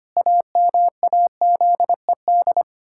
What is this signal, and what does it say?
Morse code, heard as a keyed tone, about 25 wpm, 700 Hz. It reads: AMAZED
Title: Pangram Challenge v2 - 25wpm